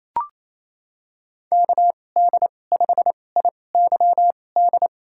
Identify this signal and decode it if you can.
Morse code, heard as a keyed tone, about 28 wpm, spaced as for 20 wpm, 700 Hz. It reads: KD5IYD
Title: USA Call Signs - 6 Characters - 2x Character Spacing - 28wpm